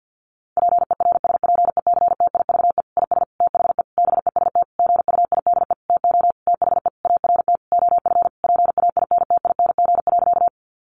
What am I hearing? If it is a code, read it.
Morse code, heard as a keyed tone, about 50 wpm, 700 Hz. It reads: PERSPECTIVE IS THE BEST GUIDE TO THE ART OF PAINTING.
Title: Sentences from Top 1200 Words 50wpm